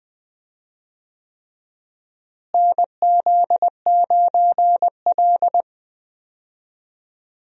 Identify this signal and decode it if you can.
Morse code, heard as a keyed tone, about 20 wpm, 700 Hz. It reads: NZ9L
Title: USA Call Signs - 4 Characters - Repeated 3x - 20wpm